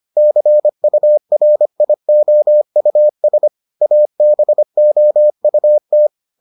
Morse code, heard as a keyed tone, about 25 wpm, 600 Hz. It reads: CURIOUS ABOUT